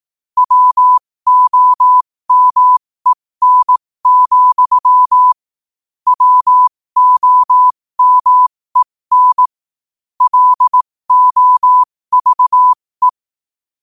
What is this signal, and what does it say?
Morse code, heard as a keyed tone, about 18 wpm, spaced as for 15 wpm, 1000 Hz. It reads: WOMEN, WOMEN LOVE
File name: qotdfiles / 15wpm